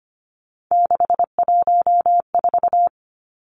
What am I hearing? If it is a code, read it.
Morse code, heard as a keyed tone, about 25 wpm, 700 Hz. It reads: 614
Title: Three Challenging Letter-Number Combinations 25wpm